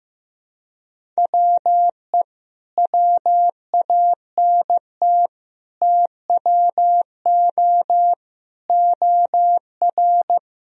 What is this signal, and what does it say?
Morse code, heard as a keyed tone, about 15 wpm, 700 Hz. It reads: WE WANT TWO OR
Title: Sentences from Top 1000 Words - Review Part 1 - 15wpm